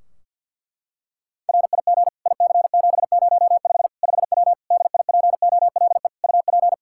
Morse code, heard as a keyed tone, about 50 wpm, 700 Hz. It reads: RIG IC705 5W DIPOLE UP
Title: K1USN SST QSO Practice 50wpm